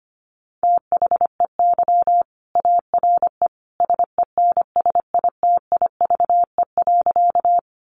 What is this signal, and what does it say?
Morse code, heard as a keyed tone, about 25 wpm, 700 Hz. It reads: THEY ARE SENSITIVE.
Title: Sentences from Top 1400 Words 25wpm